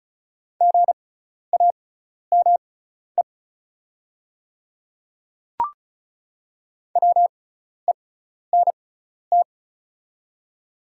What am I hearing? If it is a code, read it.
Morse code, heard as a keyed tone, about 36 wpm, spaced as for 12 wpm, 690 Hz. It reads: GAME WENT
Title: 6x Character Spacing - Top 500 Words - 35wpm